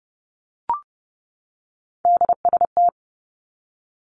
Morse code, heard as a keyed tone, about 30 wpm, 700 Hz. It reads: DST
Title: Three Challenging Letter-Number Combinations 30wpm